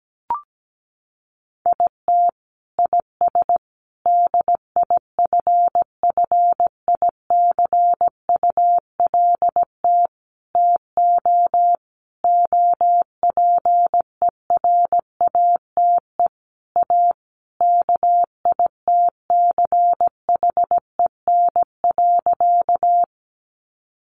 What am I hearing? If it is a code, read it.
Morse code, heard as a keyed tone, about 17 wpm, 700 Hz. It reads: IT IS DIFFICULT TO OPERATE A KITCHEN.
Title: Sentences from Top 1500 Words 17wpm